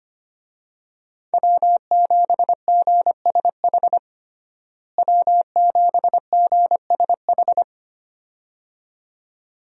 Morse code, heard as a keyed tone, about 25 wpm, 700 Hz. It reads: W7GSH W7GSH
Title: USA Call Signs - 5 Characters - Repeated 2x - 25wpm